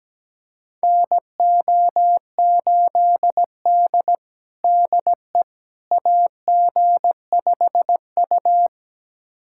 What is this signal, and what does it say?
Morse code, heard as a keyed tone, about 17 wpm, 700 Hz. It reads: NO8D DE AG5U
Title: USA Callsign DE Callsign - 4 Characters - 17wpm